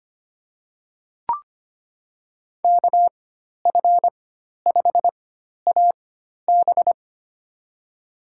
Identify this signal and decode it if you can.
Morse code, heard as a keyed tone, about 25 wpm, spaced as for 12 wpm, 700 Hz. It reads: KF5AB
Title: USA Call Signs - 5 Characters - 4x Character Spacing - 25wpm